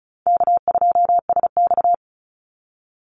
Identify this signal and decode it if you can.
Morse code, heard as a keyed tone, about 34 wpm, 700 Hz. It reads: K2SX
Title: Call Signs Repeated 3x - 35wpm